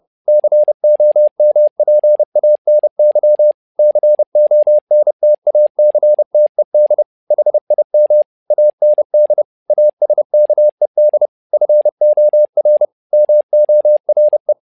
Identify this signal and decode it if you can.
Morse code, heard as a keyed tone, about 30 wpm, 600 Hz. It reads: COMPANY CONTACTED HIM AND ASKED FOR MORE
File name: learning center / tower trouble / tower trouble 30wpm